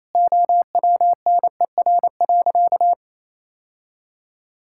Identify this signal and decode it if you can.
Morse code, heard as a keyed tone, about 28 wpm, 700 Hz. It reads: OWNER.
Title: Sentences from Top 1600 Words 28wpm